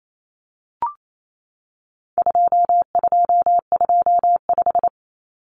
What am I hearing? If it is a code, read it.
Morse code, heard as a keyed tone, about 28 wpm, 700 Hz. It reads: 2225
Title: Common Cut Numbers - 4-Digit 28wpm